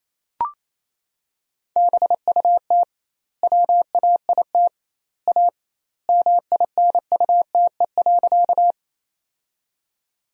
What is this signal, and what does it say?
Morse code, heard as a keyed tone, about 28 wpm, 700 Hz. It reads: BUT WAIT A MINUTE.
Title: 2x Word-Spacing - Sentences from Top 500 Words - 28wpm